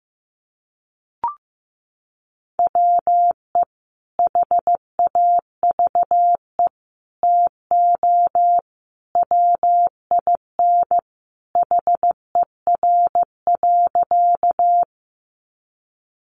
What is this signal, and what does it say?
Morse code, heard as a keyed tone, about 15 wpm, 700 Hz. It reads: WE HAVE TO WIN HER.
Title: Sentences from Top 1500 Words 15wpm